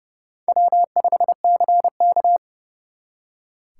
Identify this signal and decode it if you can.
Morse code, heard as a keyed tone, about 37 wpm, 700 Hz. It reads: W5CK
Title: Speed Racing - USA 1x2 Call Signs 30wpm